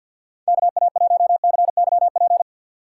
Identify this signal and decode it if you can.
Morse code, heard as a keyed tone, about 46 wpm, 700 Hz. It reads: KA1CYP
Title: USA Call Signs - 6 Characters - 4x Character Spacing - 50wpm